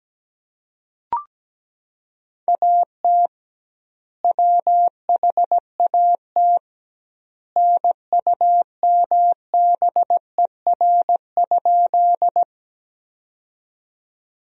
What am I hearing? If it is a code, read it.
Morse code, heard as a keyed tone, about 17 wpm, 700 Hz. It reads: AT WHAT NUMBER?
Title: 2x Word-Spacing - Sentences from Top 100 Words - 17wpm